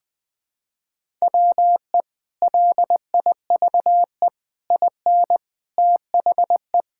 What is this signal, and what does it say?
Morse code, heard as a keyed tone, about 20 wpm, 700 Hz. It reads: WE LIVE IN THE
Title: Sentences with 3-Word Phrases - 20wpm